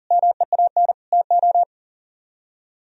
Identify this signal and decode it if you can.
Morse code, heard as a keyed tone, about 48 wpm, 700 Hz. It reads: MEAN TO
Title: Speed-Racing - Sets of 5 Words 40wpm